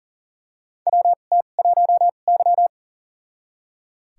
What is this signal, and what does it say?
Morse code, heard as a keyed tone, about 40 wpm, spaced as for 29 wpm, 690 Hz. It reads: WT1Y
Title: USA Call Signs - 4 Characters - 2x Character Spacing - 40wpm